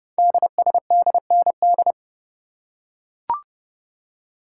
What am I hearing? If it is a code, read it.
Morse code, heard as a keyed tone, about 30 wpm, 700 Hz. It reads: DSDND
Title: Code Groups - 5 Letters - 30wpm